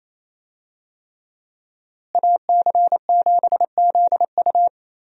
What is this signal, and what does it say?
Morse code, heard as a keyed tone, about 27 wpm, 700 Hz. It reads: AC7ZU
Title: USA Call Signs - 5 Characters - Repeated 3x - 28wpm